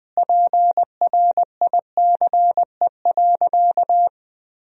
Morse code, heard as a keyed tone, about 21 wpm, 700 Hz. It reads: PRICE.
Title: Sentences from Top 1000 Words - Review Part 4 - 20wpm